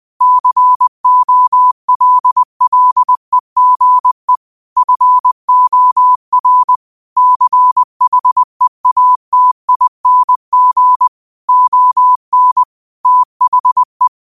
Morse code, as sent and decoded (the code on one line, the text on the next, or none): -.-. --- .-.. .-.. --. / ..-. --- .-. / -.-. .... .- - .. -. --. / --- -. / - .... .
COLLEGE FOR CHEATING ON THE